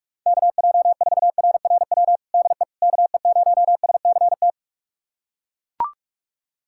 Morse code, heard as a keyed tone, about 42 wpm, 700 Hz. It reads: KJ4RRW DE KE0SCT
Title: USA Callsign DE Callsign - 6 Characters - 45wpm